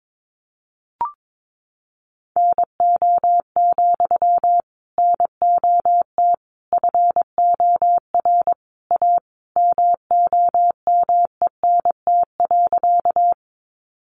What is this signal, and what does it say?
Morse code, heard as a keyed tone, about 22 wpm, 700 Hz. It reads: NO, NOT FOR A MOMENT.
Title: Sentences from Top 1000 Words 22wpm